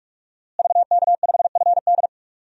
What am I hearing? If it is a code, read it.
Morse code, heard as a keyed tone, about 44 wpm, 700 Hz. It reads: VK5FD
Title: DXCC-100 Call Signs - 5 Characters - Repeated 3x - 45wpm